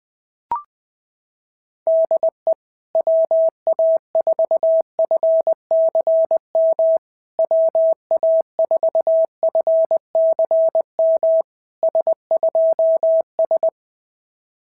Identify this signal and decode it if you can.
Morse code, heard as a keyed tone, about 20 wpm, 650 Hz. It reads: DE WA4FCM WA4FCM S2S
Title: SOTA QSO Practice 20wpm